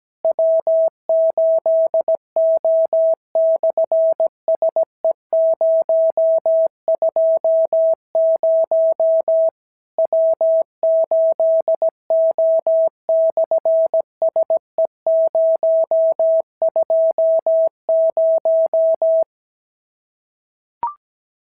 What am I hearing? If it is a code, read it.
Morse code, heard as a keyed tone, about 17 wpm, 650 Hz. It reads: W8O/SE020 W8O/SE020
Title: SOTA QSO Practice 17wpm